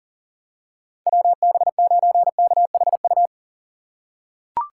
Morse code, heard as a keyed tone, about 40 wpm, 700 Hz. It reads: WB9KHU
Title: USA Call Signs - 6 Characters - 2x Character Spacing - 40wpm